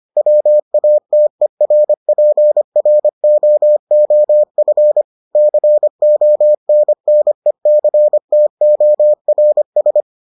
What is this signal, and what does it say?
Morse code, heard as a keyed tone, about 25 wpm, 600 Hz. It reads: WATERPROOF CONNECTORS